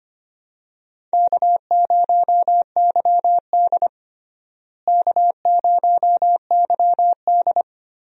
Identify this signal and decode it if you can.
Morse code, heard as a keyed tone, about 25 wpm, 700 Hz. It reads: K0YD K0YD
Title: USA Call Signs - 4 Characters - Repeated 3x - 25wpm